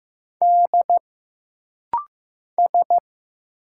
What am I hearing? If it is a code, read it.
Morse code, heard as a keyed tone, about 15 wpm, 700 Hz. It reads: D S
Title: Single Letters - Rapid-Fire 15wpm